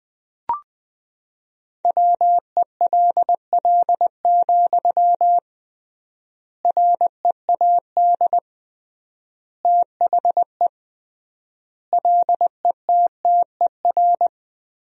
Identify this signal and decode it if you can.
Morse code, heard as a keyed tone, about 20 wpm, 700 Hz. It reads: WELL, READ THE LETTER
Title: 3x Word-Spacing - Sentences from Top 300 Words - 20wpm